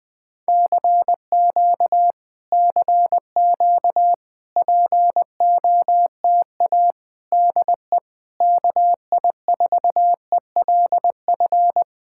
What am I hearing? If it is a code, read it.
Morse code, heard as a keyed tone, about 21 wpm, 700 Hz. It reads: CQ CQ POTA DE KI4ELF
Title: Speed-Racing Common QSO Phrases - 20wpm